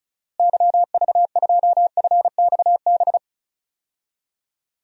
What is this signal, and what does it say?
Morse code, heard as a keyed tone, about 34 wpm, 700 Hz. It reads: YV2FXB
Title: DXCC-100 Call Signs - 6 Characters - Repeated 3x - 35wpm